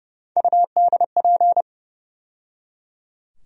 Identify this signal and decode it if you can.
Morse code, heard as a keyed tone, about 30 wpm, 700 Hz. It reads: UDP
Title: USA Call Sign Postfixes - 3 Letters 30wpm